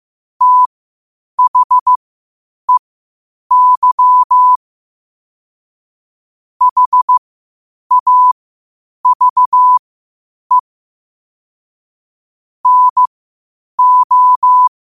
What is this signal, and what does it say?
Morse code, heard as a keyed tone, about 15 wpm, spaced as for 8 wpm, 1000 Hz. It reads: THEY HAVE NO
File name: qotdfiles / 8wpm